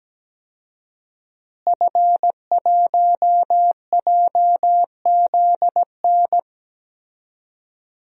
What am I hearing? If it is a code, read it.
Morse code, heard as a keyed tone, about 17 wpm, 700 Hz. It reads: F1JZN
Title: DXCC-100 Call Signs - 5 Characters - Repeated 2x - 17wpm